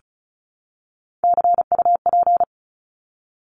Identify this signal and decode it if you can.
Morse code, heard as a keyed tone, about 34 wpm, 700 Hz. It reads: CUP
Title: Top 1501-1750 Words 35wpm